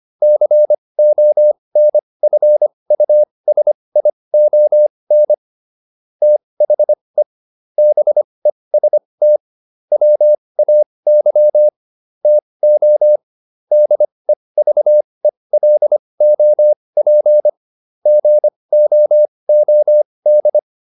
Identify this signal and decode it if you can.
Morse code, heard as a keyed tone, about 25 wpm, spaced as for 20 wpm, 600 Hz. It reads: CONFUSION THE BEST WAY TO DEVELOP GOOD